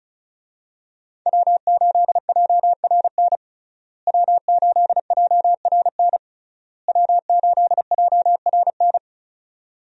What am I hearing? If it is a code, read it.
Morse code, heard as a keyed tone, about 36 wpm, 700 Hz. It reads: W8JRN W8JRN W8JRN
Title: USA Call Signs - 5 Characters - Repeated 3x - 35wpm